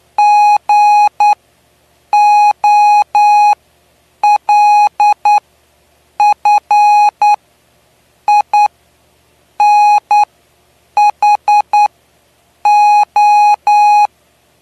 Morse code, heard as a keyed tone, about 9 wpm, spaced as for 7 wpm, 830 Hz. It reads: GOLFINHO
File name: cw / py2lcd / 09-PY2LCD-Li��o 9